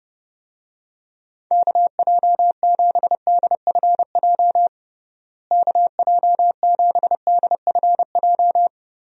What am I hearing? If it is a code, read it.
Morse code, heard as a keyed tone, about 30 wpm, 700 Hz. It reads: KJ7DFJ KJ7DFJ
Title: USA Call Signs - 6 Characters - Repeated 2x - 30wpm